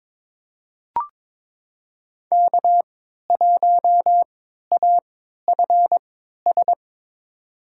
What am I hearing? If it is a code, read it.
Morse code, heard as a keyed tone, about 21 wpm, spaced as for 12 wpm, 700 Hz. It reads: K1AFS
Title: USA Call Signs - 5 Characters - 3x Character Spacing - 22wpm